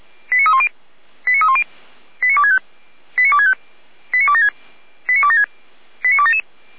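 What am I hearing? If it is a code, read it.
Five-tone selective call (ZVEI), heard as quick runs of five tones, tones of about 70 ms. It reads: 89310 89310 89265 89265 89276 89276 89277
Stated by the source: ZVEI five-tone selective call received on an SDR, from the Feuerwehr Orlamünde audio server.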